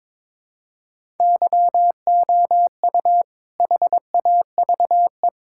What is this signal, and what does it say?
Morse code, heard as a keyed tone, about 22 wpm, 700 Hz. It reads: YOU HAVE
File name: practice / 22wpm / Sentences with 4-Word Phrases - 22wpm